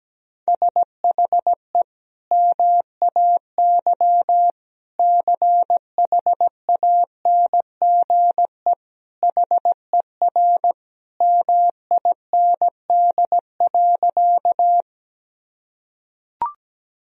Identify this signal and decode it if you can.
Morse code, heard as a keyed tone, about 17 wpm, 700 Hz. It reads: SHE MAY CHANGE HER MIND.
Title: Sentences from Top 1000 Words - Review Part 1 - 17wpm